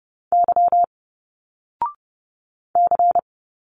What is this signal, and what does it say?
Morse code, heard as a keyed tone, about 30 wpm, 700 Hz. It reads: Y C
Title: Single Letters 30wpm